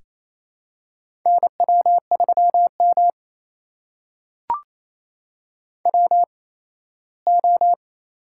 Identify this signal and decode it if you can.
Morse code, heard as a keyed tone, about 27 wpm, spaced as for 17 wpm, 700 Hz. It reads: NW3M WO
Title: USA Call Signs - 4 Characters - 8x Character Spacing - 28wpm